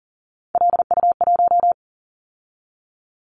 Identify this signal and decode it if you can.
Morse code, heard as a keyed tone, about 40 wpm, 700 Hz. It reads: LU1
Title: Callsign Prefixes for Top 100 DXCC - 40wpm